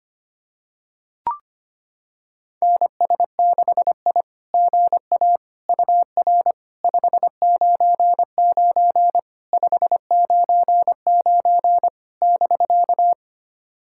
Morse code, heard as a keyed tone, about 25 wpm, 660 Hz. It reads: NS6I GA UR 599 599 <BK>
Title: SOTA QSO Practice 25wpm